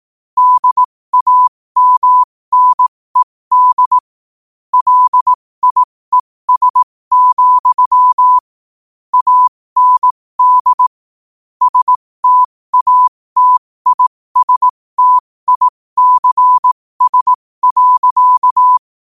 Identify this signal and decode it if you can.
Morse code, heard as a keyed tone, about 18 wpm, spaced as for 15 wpm, 1000 Hz. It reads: DAMNED LIES, AND STATISTICS.
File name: qotdfiles / 15wpm